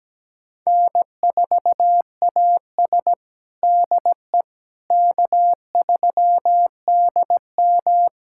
Morse code, heard as a keyed tone, about 17 wpm, 700 Hz. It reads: N4AS DE K3DM
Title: USA Callsign DE Callsign - 4 Characters - 17wpm